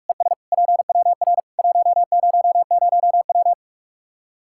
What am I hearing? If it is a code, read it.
Morse code, heard as a keyed tone, about 44 wpm, 700 Hz. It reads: ES PWR 100W
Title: Practice QSO - Exchange 2 - Info - 45wpm